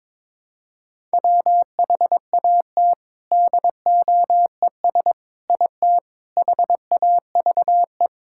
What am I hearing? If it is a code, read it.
Morse code, heard as a keyed tone, about 22 wpm, 700 Hz. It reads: WHAT DOES IT HAVE
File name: practice / 22wpm / Sentences with 4-Word Phrases - 22wpm